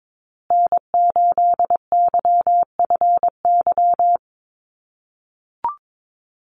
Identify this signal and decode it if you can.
Morse code, heard as a keyed tone, about 22 wpm, 700 Hz. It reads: N8YFY
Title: USA Call Signs Repeated 2x 22wpm